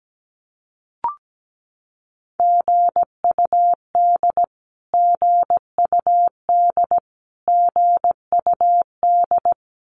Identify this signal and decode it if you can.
Morse code, heard as a keyed tone, about 17 wpm, 700 Hz. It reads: GUD GUD GUD
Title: Abbreviations Repeated 3x - 17wpm